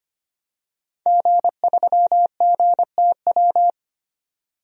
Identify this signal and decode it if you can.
Morse code, heard as a keyed tone, about 25 wpm, 700 Hz. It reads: G3GTW
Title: DXCC-100 Call Signs - 5 Characters - 25wpm